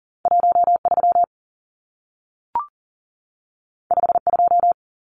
Morse code, heard as a keyed tone, about 40 wpm, 700 Hz. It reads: 13 52
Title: Numbers - 2-Digit 40wpm